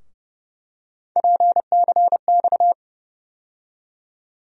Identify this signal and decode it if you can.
Morse code, heard as a keyed tone, about 30 wpm, 700 Hz. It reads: PCX
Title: USA Call Sign Postfixes - 3 Letters 30wpm